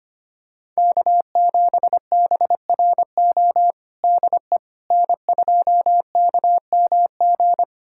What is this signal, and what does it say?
Morse code, heard as a keyed tone, about 25 wpm, 700 Hz. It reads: K7BRO DE N2KMG
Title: USA Callsign DE Callsign - 5 Characters - 25wpm